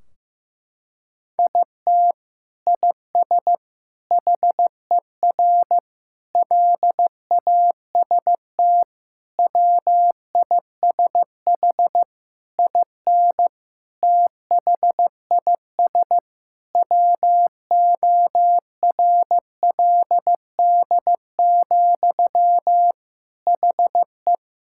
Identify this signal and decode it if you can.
Morse code, heard as a keyed tone, about 15 wpm, 700 Hz. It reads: IT IS HER LAST WISH IN THIS WORLD, HE
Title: Sentences from Top 1600 Words 15wpm